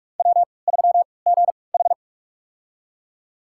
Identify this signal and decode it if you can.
Morse code, heard as a keyed tone, about 47 wpm, spaced as for 26 wpm, 700 Hz. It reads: W3GH
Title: USA Call Signs - 4 Characters - 3x Character Spacing - 45wpm